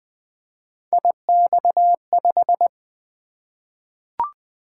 Morse code, heard as a keyed tone, about 20 wpm, 700 Hz. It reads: IX5
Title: Three Letter-Number Combinations - 20wpm